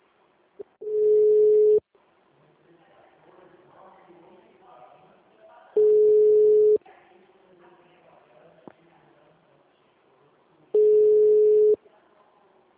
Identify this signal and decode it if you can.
Ringback tone (Europe, 425 Hz), with speech.